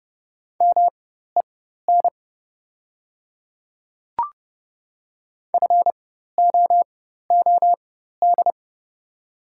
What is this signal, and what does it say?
Morse code, heard as a keyed tone, about 30 wpm, spaced as for 14 wpm, 700 Hz. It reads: MEN FOOD
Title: 4x Character Spacing - Top 300 Words - 30wpm